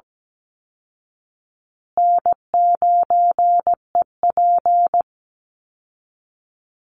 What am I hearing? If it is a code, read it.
Morse code, heard as a keyed tone, about 17 wpm, 700 Hz. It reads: N9EP
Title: Easy Call Signs Repeated 3x - 17wpm